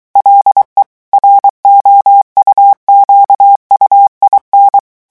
Morse code, heard as a keyed tone, about 23 wpm, 800 Hz. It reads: LE ROUQUIN
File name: morse mp3 / revision9 citations humoristiques vitesse 22 mots minute